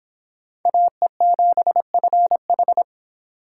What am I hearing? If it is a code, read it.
Morse code, heard as a keyed tone, about 25 wpm, 700 Hz. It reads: AE7FH